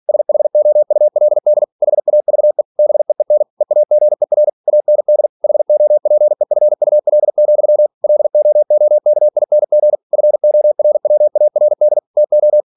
Morse code, heard as a keyed tone, about 46 wpm, 600 Hz. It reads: SHOULD HAVE BEEN EAGER AND HOPEFUL, LOOKING FORWARD TO
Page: Never 47wpm